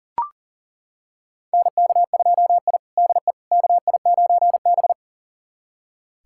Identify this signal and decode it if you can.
Morse code, heard as a keyed tone, about 40 wpm, 700 Hz. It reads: NK2I DE KI9B
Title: USA Callsign DE Callsign - 4 Characters - 40wpm